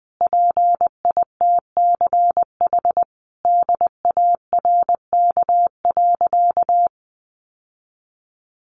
Morse code, heard as a keyed tone, about 19 wpm, 700 Hz. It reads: PITCH DARK.
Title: Sentences from Top 1300 Words 20wpm